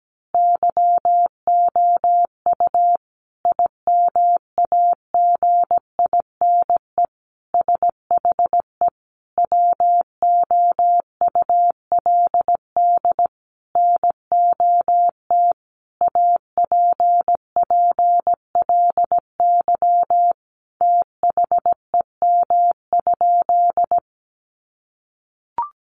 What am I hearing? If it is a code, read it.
Morse code, heard as a keyed tone, about 17 wpm, 700 Hz. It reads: YOU IMAGINE SHE WOULD NOT APPLY THEM?
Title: Sentences from Top 1400 Words 17wpm